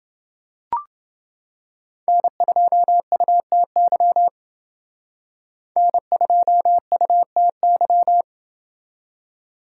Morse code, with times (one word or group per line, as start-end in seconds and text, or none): 2.1-4.3 N2UTY
5.8-8.2 N2UTY